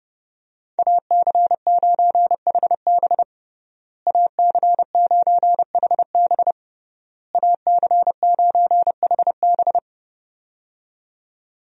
Morse code, heard as a keyed tone, about 30 wpm, 700 Hz. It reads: AC9HB AC9HB AC9HB
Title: USA Call Signs - 5 Characters - Repeated 3x - 30wpm